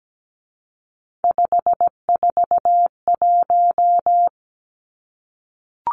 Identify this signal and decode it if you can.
Morse code, heard as a keyed tone, about 17 wpm, 700 Hz. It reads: 541
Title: Numbers - 3-Digit 17wpm